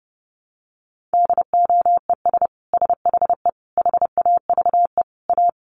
Morse code, heard as a keyed tone, about 30 wpm, 700 Hz. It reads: DOES SHE HAVE A
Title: Sentences from Top 1500 Words 30wpm